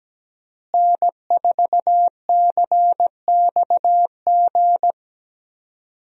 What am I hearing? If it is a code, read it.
Morse code, heard as a keyed tone, about 17 wpm, 700 Hz. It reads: N4CXG
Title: USA Call Signs - 5 Characters - Repeated 3x - 17wpm